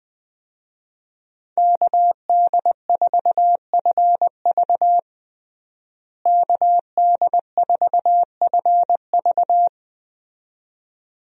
Morse code, heard as a keyed tone, about 20 wpm, 700 Hz. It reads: KD4FV KD4FV
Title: USA Call Signs - 5 Characters - Repeated 2x - 20wpm